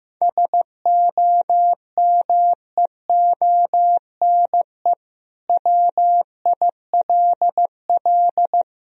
Morse code, heard as a keyed tone, about 16 wpm, 700 Hz. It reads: SOMEONE WILL
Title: Sentences from Top 1000 Words - Review Part 1 - 15wpm